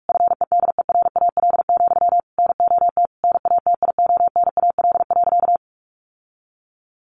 Morse code, heard as a keyed tone, about 44 wpm, 700 Hz. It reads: FEDERAL, NOT NATIONAL.
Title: Sentences from Top 1200 Words 45wpm